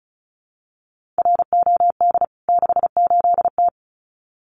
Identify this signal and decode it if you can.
Morse code, heard as a keyed tone, about 34 wpm, 700 Hz. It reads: ROD 68T
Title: CWT Exchange 35wpm